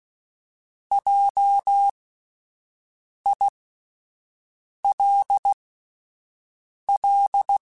Morse code, heard as a keyed tone, about 16 wpm, spaced as for 5 wpm, 800 Hz. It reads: JILL
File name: Spaced Alphas / Spaced Alphas mp3s / Js sp @10